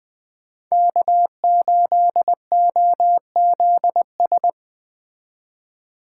Morse code, heard as a keyed tone, about 20 wpm, 700 Hz. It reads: K8OZS